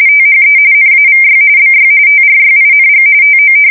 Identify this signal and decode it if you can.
RTTY, about 45.45 baud. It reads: THE NAVY BY TRYING TO
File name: fsk170 / usnavy4